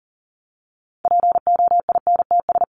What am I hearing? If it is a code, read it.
Morse code, heard as a keyed tone, about 40 wpm, 700 Hz. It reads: POINTS
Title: ES:S Suffix - Top 100 Words with Base Words 40wpm